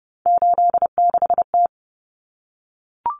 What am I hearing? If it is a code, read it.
Morse code, heard as a keyed tone, about 30 wpm, 700 Hz. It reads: 86T
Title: Common Cut Numbers - 3-Digit 30wpm